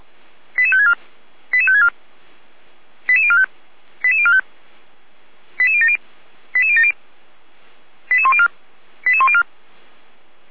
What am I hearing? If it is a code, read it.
Five-tone selective call (ZVEI), heard as quick runs of five tones, tones of about 70 ms. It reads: 88564 88564 88045 88045 88070 88070 88104 88104